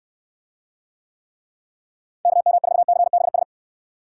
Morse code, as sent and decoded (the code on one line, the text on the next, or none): -.- -- ....- -.-. -... ...
KM4CBS